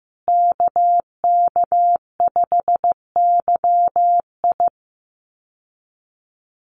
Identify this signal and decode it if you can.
Morse code, heard as a keyed tone, about 15 wpm, 700 Hz. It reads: KK5YI
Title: USA Call Signs Repeated 2x 15wpm